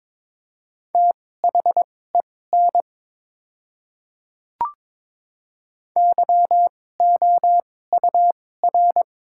Morse code, heard as a keyed tone, about 22 wpm, spaced as for 16 wpm, 700 Hz. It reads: THEN YOUR
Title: 2x Character Spacing - Top 100 Words - 22wpm